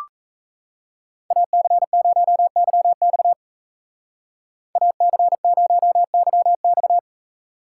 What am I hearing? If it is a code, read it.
Morse code, heard as a keyed tone, about 34 wpm, 700 Hz. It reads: AC0YX AC0YX
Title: Speed Racing - USA 2x2 Call Signs 28wpm